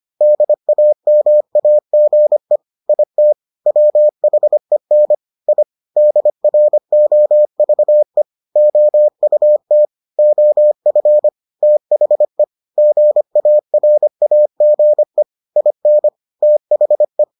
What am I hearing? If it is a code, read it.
Morse code, heard as a keyed tone, about 25 wpm, 600 Hz. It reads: DAMAGE IT WHEN I DROVE OUT OF THE GARAGE IN THE